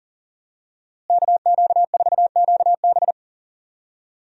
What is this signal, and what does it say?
Morse code, heard as a keyed tone, about 40 wpm, 700 Hz. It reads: KQ4QB
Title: USA Call Signs - 5 Characters - Repeated 2x - 40wpm